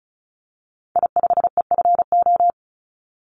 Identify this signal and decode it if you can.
Morse code, heard as a keyed tone, about 34 wpm, 700 Hz. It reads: I5EFO
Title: Call Signs Repeated 2x - 35wpm